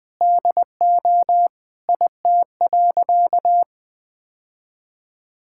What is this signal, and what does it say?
Morse code, heard as a keyed tone, about 20 wpm, 700 Hz. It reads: DO IT.
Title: Sentences with 3-Word Phrases - Spoken Beginning - 20wpm